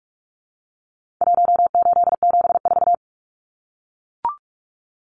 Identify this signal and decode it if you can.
Morse code, heard as a keyed tone, about 44 wpm, 700 Hz. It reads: 1874
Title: Numbers - Contest Style - 4-Digit 45wpm